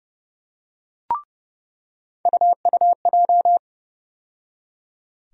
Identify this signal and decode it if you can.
Morse code, heard as a keyed tone, about 30 wpm, 700 Hz. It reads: UUJ
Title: USA Call Sign Postfixes - 3 Letters 30wpm